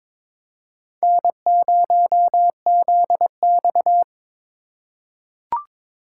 Morse code, heard as a keyed tone, about 22 wpm, 700 Hz. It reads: N0ZX
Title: USA Call Signs - 4 Characters - 6x Character Spacing - 22wpm